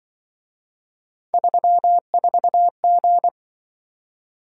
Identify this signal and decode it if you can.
Morse code, heard as a keyed tone, about 28 wpm, 700 Hz. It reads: 34G